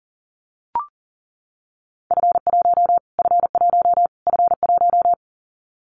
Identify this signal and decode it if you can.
Morse code, heard as a keyed tone, about 40 wpm, 700 Hz. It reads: F1 F1 F1
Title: Callsign Prefixes for Top 25 DXCC - Repeated 3x - 40wpm